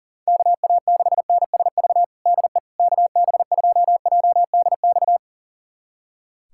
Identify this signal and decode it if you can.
Morse code, heard as a keyed tone, about 40 wpm, 700 Hz. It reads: KA6NSV DE KB2JDX
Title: USA Callsign DE Callsign 40wpm